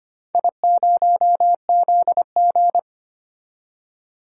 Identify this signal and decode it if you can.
Morse code, heard as a keyed tone, about 25 wpm, 700 Hz. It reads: I0ZG
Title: DXCC-100 Call Signs - 4 Characters - 25wpm